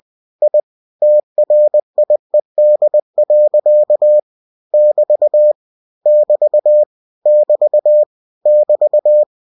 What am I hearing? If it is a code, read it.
Morse code, heard as a keyed tone, about 20 wpm, 600 Hz. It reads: I TRIED. <BT> <BT> <BT> <BT>